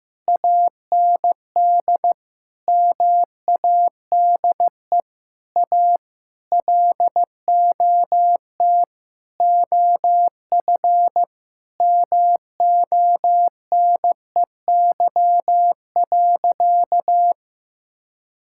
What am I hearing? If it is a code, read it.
Morse code, heard as a keyed tone, about 15 wpm, 700 Hz. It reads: AND MADE A LOT OF MONEY.
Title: Sentences from Top 1000 Words - Review Part 6 - 15wpm